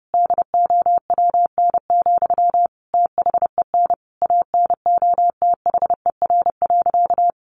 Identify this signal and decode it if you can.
Morse code, heard as a keyed tone, about 30 wpm, 700 Hz. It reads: DOWN, THEN ANOTHER.
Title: Sentences from Top 1400 Words 30wpm